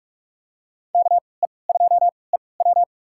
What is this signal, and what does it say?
Morse code, heard as a keyed tone, about 44 wpm, spaced as for 25 wpm, 700 Hz. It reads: KE2EW
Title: USA Call Signs - 5 Characters - 3x Character Spacing - 45wpm